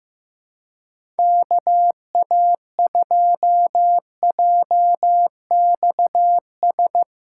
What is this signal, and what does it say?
Morse code, heard as a keyed tone, about 15 wpm, 700 Hz. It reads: KA2JXS
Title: USA Callsign DE Callsign - 6 Characters - 15wpm